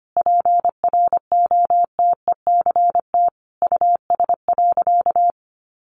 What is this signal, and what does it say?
Morse code, heard as a keyed tone, about 25 wpm, 700 Hz. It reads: PROTECT US.
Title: Sentences from Top 900 Words 25wpm